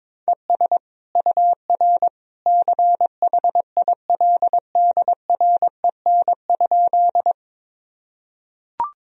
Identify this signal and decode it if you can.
Morse code, heard as a keyed tone, about 22 wpm, 700 Hz. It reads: ES UR CHILDREN?
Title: Sentences from Top 300 Words and Abbreviations 22wpm